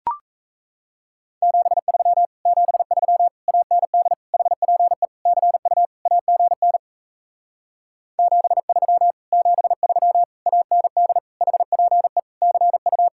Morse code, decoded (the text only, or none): 73 73 AND HPE CU AGN 73 73 AND HPE CU